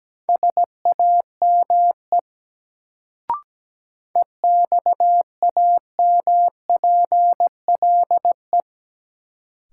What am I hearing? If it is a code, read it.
Morse code, heard as a keyed tone, about 17 wpm, 700 Hz. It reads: SAME EXAMPLE
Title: Sets of 2 Words - New Word Drill - 17wpm